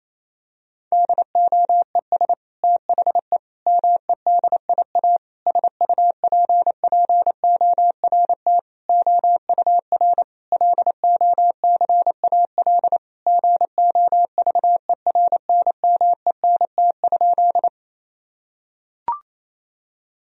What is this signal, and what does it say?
Morse code, heard as a keyed tone, about 28 wpm, 700 Hz. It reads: DOES THE MEDIA SUPPORT OUR LOCAL GOVERNMENT?
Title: Sentences from Top 1000 Words - Review Part 4 - 28wpm